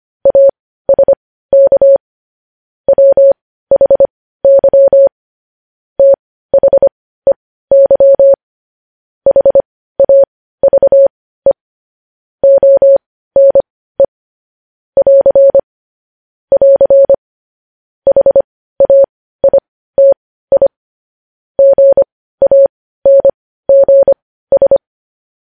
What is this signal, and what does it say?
Morse code, heard as a keyed tone, about 25 wpm, spaced as for 15 wpm, 550 Hz. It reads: ASK WHY THEY HAVE ONE <AR> <AR> HAITI GANGS
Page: news-headlines-in-morse-code-at-15-wpm